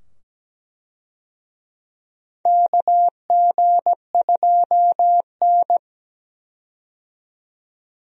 Morse code, heard as a keyed tone, about 17 wpm, 700 Hz. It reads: KG2N
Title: USA Call Signs - 4 Characters - 17wpm